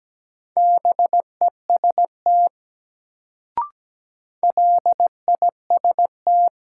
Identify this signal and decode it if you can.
Morse code, heard as a keyed tone, about 17 wpm, 700 Hz. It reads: BEST LIST
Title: Sentences from Top 400 Words - New Word Drill - 17wpm